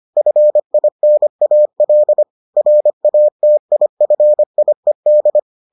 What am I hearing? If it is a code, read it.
Morse code, heard as a keyed tone, about 25 wpm, 600 Hz. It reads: FINAL RATIFIED